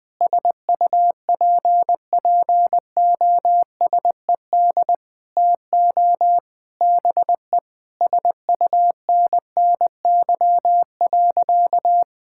Morse code, heard as a keyed tone, about 20 wpm, 700 Hz. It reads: SUPPOSED TO BE SUNNY.
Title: Sentences with 3-Word Phrases - 20wpm